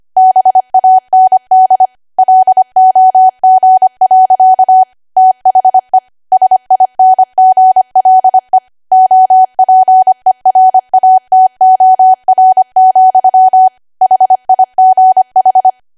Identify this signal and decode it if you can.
Morse code, heard as a keyed tone, about 25 wpm, 750 Hz. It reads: BAND LOG. THE SINGLE OPERATOR, HIGH